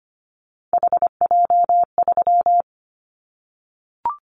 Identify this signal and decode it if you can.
Morse code, heard as a keyed tone, about 25 wpm, 700 Hz. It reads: HJ3
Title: Callsign Prefixes for Top 100 DXCC - 25wpm